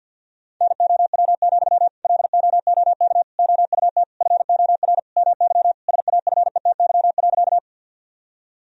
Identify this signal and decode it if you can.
Morse code, heard as a keyed tone, about 49 wpm, 700 Hz. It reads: NOW, LOOK OUT FOR MY SAFETY.
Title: Sentences from Top 1000 Words - Review Part 4 - 50wpm